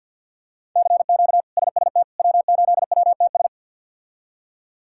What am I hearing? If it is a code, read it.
Morse code, heard as a keyed tone, about 52 wpm, 700 Hz. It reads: CQ SST W8WTS
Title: K1USN SST QSO Practice - 50wpm